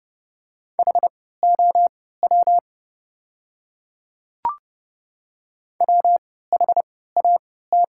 Morse code, heard as a keyed tone, about 30 wpm, spaced as for 17 wpm, 700 Hz. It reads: HOW WHAT
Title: 3x Character Spacing - Top 300 Words - 30wpm